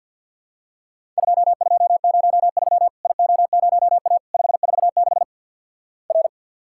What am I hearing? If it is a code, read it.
Morse code, heard as a keyed tone, about 50 wpm, 700 Hz. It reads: 2203 IO0A 546 R
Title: Worked All Europe DX Contest - QTC Exchange with Repeats - 50wpm